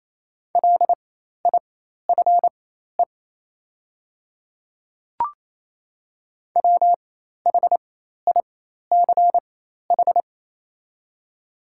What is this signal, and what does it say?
Morse code, heard as a keyed tone, about 28 wpm, spaced as for 13 wpm, 700 Hz. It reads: LIFE WHICH
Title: 4x Character Spacing - Top 300 Words - 28wpm